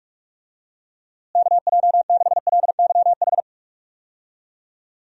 Morse code, heard as a keyed tone, about 41 wpm, 700 Hz. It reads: KJ6LYH
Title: USA Call Signs - 6 Characters - Repeated 2x - 45wpm